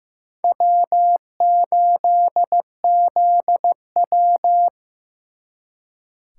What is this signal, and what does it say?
Morse code, heard as a keyed tone, about 18 wpm, 700 Hz. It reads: W8ZW
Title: Speed Racing - USA 1x2 Call Signs 15wpm